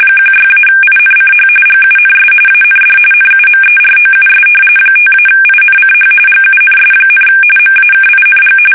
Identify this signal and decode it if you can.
RTTY, about 45.45 baud. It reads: 16-INCH GUNS AND NUMEROUS 5-INCH GUNS, MAKING
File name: fsk850 / battleshipnj